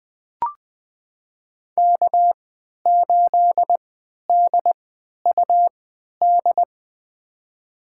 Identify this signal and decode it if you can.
Morse code, heard as a keyed tone, about 20 wpm, spaced as for 11 wpm, 700 Hz. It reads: K8DUD
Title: USA Call Signs - 3x Character Spacing - 20wpm